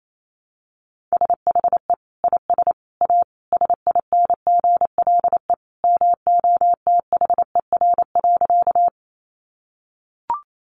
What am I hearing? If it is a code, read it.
Morse code, heard as a keyed tone, about 28 wpm, 700 Hz. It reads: SHE IS A SINGLE MOTHER.
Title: Sentences from Top 700 Words 28wpm